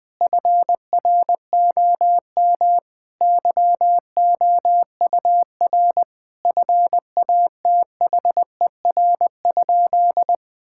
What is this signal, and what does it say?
Morse code, heard as a keyed tone, about 20 wpm, 700 Hz. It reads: FROM YOUR FATHER?
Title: Sentences from Top 1600 Words 20wpm